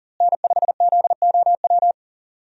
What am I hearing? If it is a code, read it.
Morse code, heard as a keyed tone, about 40 wpm, 700 Hz. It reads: N5ZOW
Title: USA Call Signs - 5 Characters - Repeated 2x - 40wpm